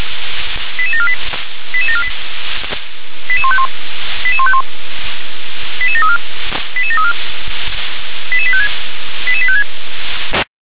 Five-tone selective call (ZVEI), heard as quick runs of five tones, tones of about 70 ms. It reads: 88748 88748 88151 88151 88734 88734 88856 88856